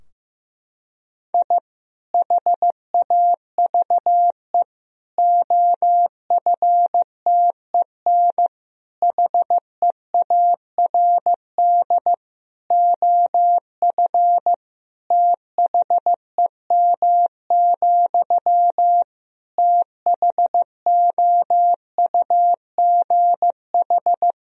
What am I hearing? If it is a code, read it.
Morse code, heard as a keyed tone, about 15 wpm, 700 Hz. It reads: I HAVE OFTEN HEARD OF THEM, THOUGH
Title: Sentences from Top 1000 Words - Review Part 2 - 15wpm